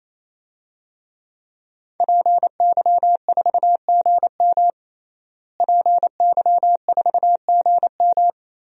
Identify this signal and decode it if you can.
Morse code, heard as a keyed tone, about 28 wpm, 700 Hz. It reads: PY4GM PY4GM
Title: DXCC-100 Call Signs - 5 Characters - Repeated 3x - 28wpm